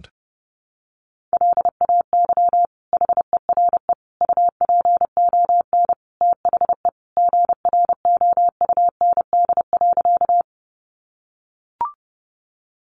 Morse code, heard as a keyed tone, about 30 wpm, 700 Hz. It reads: LAY HERE UPON THE GROUND.
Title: Sentences from Top 1100 Words 30wpm